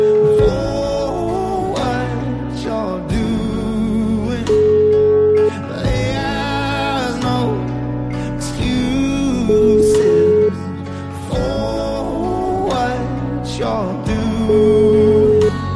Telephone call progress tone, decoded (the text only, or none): ringback tone